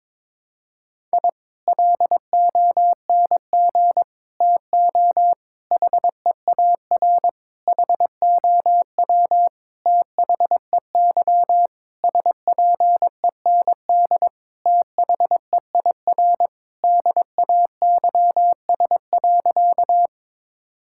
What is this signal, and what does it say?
Morse code, heard as a keyed tone, about 22 wpm, 700 Hz. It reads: I LONG TO HEAR HOW THEY SPEND THEIR DAYS.
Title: Sentences from Top 1000 Words - Review Part 5 - 22wpm